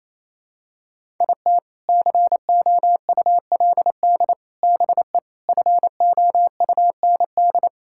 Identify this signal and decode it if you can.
Morse code, heard as a keyed tone, about 27 wpm, 700 Hz. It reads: IT COULD BE FOUND
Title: Sentences from Top 1000 Words - Review Part 5 - 28wpm